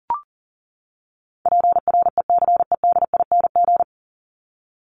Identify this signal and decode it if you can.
Morse code, heard as a keyed tone, about 40 wpm, 700 Hz. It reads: PRECEDING
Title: ING Suffix - Top 100 Words with Base Words 40wpm